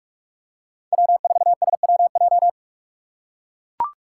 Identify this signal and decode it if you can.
Morse code, heard as a keyed tone, about 44 wpm, 700 Hz. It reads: W4SWJ